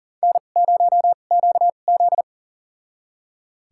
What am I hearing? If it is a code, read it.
Morse code, heard as a keyed tone, about 40 wpm, spaced as for 29 wpm, 690 Hz. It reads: N0QZ
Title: USA Call Signs - 4 Characters - 2x Character Spacing - 40wpm